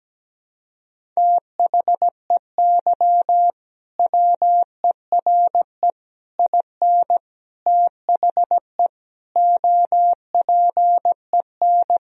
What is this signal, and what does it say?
Morse code, heard as a keyed tone, about 17 wpm, 700 Hz. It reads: THEY WERE IN THE OPEN
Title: Sentences from Top 1000 Words - Review Part 1 - 17wpm